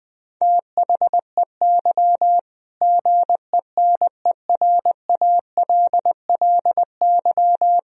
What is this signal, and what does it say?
Morse code, heard as a keyed tone, about 20 wpm, 700 Hz. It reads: THEY GENERALLY
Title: Sentences from Top 1000 Words - Review Part 4 - 20wpm